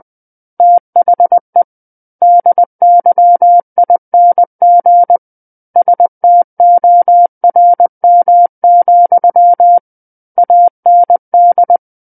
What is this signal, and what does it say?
Morse code, heard as a keyed tone, about 20 wpm, 700 Hz. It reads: THE DYING STORM, AND
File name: 20 WPM